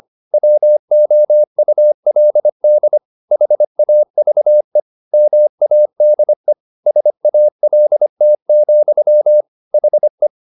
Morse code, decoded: WOULD HAVE MADE SALT, HE

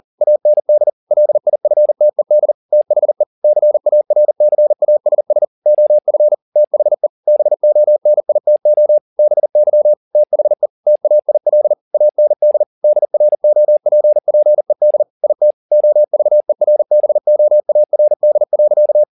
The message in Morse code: AND LIFTED THE CARCASS OF THE BONITO BY THE TAIL AND DROPPED IT OVERBOARD.

